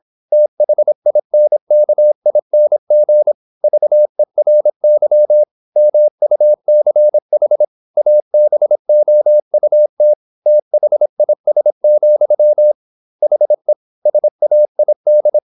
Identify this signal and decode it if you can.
Morse code, heard as a keyed tone, about 26 wpm, 600 Hz. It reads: THINKING VERY MUCH ABOUT THIS, HE SAID
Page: Never 26wpm | CW med Gnister